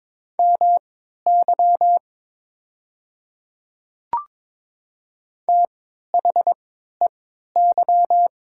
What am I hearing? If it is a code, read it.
Morse code, heard as a keyed tone, about 22 wpm, spaced as for 13 wpm, 700 Hz. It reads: MY THEY
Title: 3x Character Spacing - Top 100 Words - 22wpm